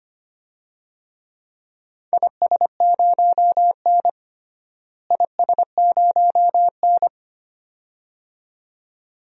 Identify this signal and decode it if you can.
Morse code, heard as a keyed tone, about 25 wpm, 700 Hz. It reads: IS0N IS0N
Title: DXCC-100 Call Signs - 4 Characters - Repeated 2x - 25wpm